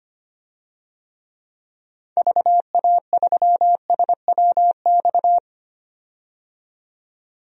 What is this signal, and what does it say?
Morse code, heard as a keyed tone, about 25 wpm, 700 Hz. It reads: VA3SWX
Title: DXCC-100 Call Signs - 6 Characters - 25wpm